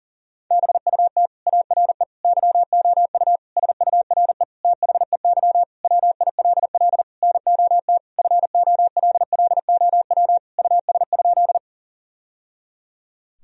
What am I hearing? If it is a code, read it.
Morse code, heard as a keyed tone, about 40 wpm, 700 Hz. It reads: BUT ARE YOU SURE THEY WILL NOT FOLLOW US?
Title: Sentences from Top 1000 Words - Review Part 1 - 40wpm